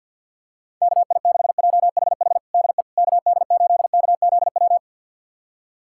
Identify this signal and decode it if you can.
Morse code, heard as a keyed tone, about 51 wpm, 700 Hz. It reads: KI6JHH DE KD8KZW
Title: USA Callsign DE Callsign 50wpm